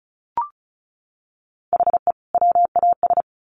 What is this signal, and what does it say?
Morse code, heard as a keyed tone, about 34 wpm, 700 Hz. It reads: HE WAS
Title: Sets of 2 Words 35wpm